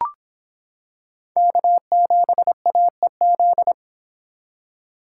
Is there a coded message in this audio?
Yes: Morse code from 1.4 to 3.7 s, heard as a keyed tone, about 25 wpm, 700 Hz. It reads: K7AEZ